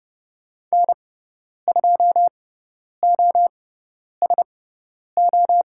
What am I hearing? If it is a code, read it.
Morse code, heard as a keyed tone, about 30 wpm, spaced as for 10 wpm, 700 Hz. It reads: N2OSO
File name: Lesson 16g - Letter c - Call Signs 30wpm